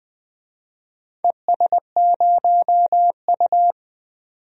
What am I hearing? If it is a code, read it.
Morse code, heard as a keyed tone, about 20 wpm, 700 Hz. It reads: ES0U
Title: DXCC-100 Call Signs - 4 Characters - 20wpm